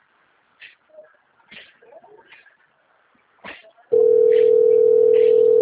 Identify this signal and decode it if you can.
Ringback tone (North America), 440 + 480 Hz.